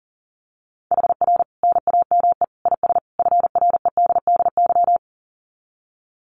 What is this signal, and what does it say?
Morse code, heard as a keyed tone, about 40 wpm, 700 Hz. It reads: HR NAME IS FREDDY
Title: Common QSO Phrases 40wpm